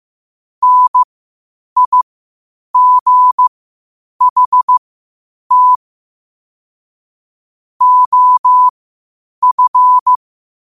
Morse code, heard as a keyed tone, about 15 wpm, spaced as for 8 wpm, 1000 Hz. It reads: NIGHT OF
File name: qotdfiles / 8wpm